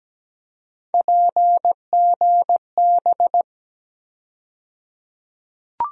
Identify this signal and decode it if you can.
Morse code, heard as a keyed tone, about 17 wpm, 700 Hz. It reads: PGB
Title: USA Call Sign Postfixes - 3 Letters 17wpm